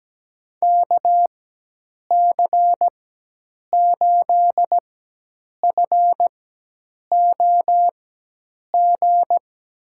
Morse code, heard as a keyed tone, about 17 wpm, spaced as for 8 wpm, 700 Hz. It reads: KC8FOG
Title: USA Call Signs - 4x Character Spacing - 17wpm